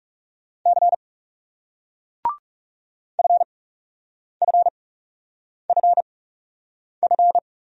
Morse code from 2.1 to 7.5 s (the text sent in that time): FFFF